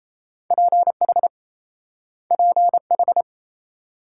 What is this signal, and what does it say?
Morse code, heard as a keyed tone, about 37 wpm, 700 Hz. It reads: PH PH